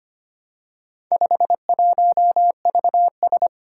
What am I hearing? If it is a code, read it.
Morse code, heard as a keyed tone, about 25 wpm, 700 Hz. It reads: 51VS
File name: practice / 25wpm / Four Letter-Number Combinations - 25wpm